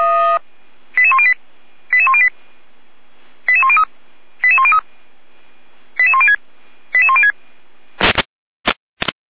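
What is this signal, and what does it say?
Five-tone selective call (ZVEI), heard as quick runs of five tones, tones of about 70 ms. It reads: 88108 88108 88103 88103 88106 88106
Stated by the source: ZVEI five-tone selective call 88190 received by SDR.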